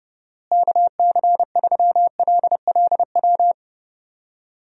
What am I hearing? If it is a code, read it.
Morse code, heard as a keyed tone, about 32 wpm, 700 Hz. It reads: KC3LLW
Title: USA Call Signs - 6 Characters - Repeated 3x - 30wpm